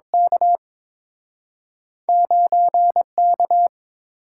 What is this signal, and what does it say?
Morse code, heard as a keyed tone, about 30 wpm, spaced as for 26 wpm, 700 Hz. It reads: K 9K
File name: Speed-Racing - Two Letter-Number Combinations - 22wpm